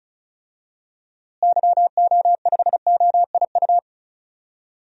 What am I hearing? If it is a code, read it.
Morse code, heard as a keyed tone, about 34 wpm, 700 Hz. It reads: YO5OIU
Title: DXCC-100 Call Signs - 6 Characters - Repeated 3x - 35wpm